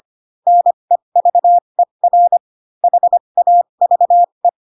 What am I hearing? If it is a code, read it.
Morse code, heard as a keyed tone, about 25 wpm, spaced as for 22 wpm, 700 Hz. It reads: NEVER HAVE
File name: Pinocchio / Pinocchio ch7 18 25 ik1hge